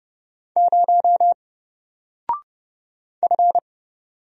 Morse code, heard as a keyed tone, about 30 wpm, 700 Hz. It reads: 0 F
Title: Single Character (Letter:Number:Punctuation:Prosign) Alt 30wpm